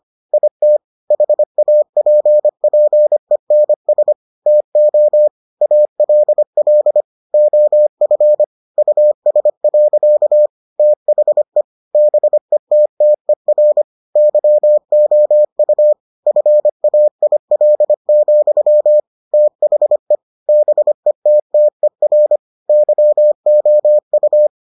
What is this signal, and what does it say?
Morse code, heard as a keyed tone, about 25 wpm, 600 Hz. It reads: IT HAPPENS TO ALL OF US. THE BETTER YOU FAIL, THE BETTER YOU